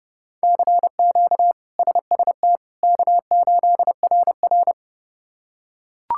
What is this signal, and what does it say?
Morse code, heard as a keyed tone, about 29 wpm, 700 Hz. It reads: CQ SST K8RR
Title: K1USN SST QSO Practice - 30wpm